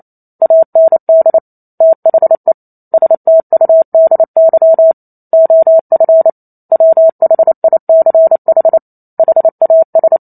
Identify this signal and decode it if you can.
Morse code, heard as a keyed tone, about 28 wpm, 650 Hz. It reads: AND THE STUDY OF WHICH HAS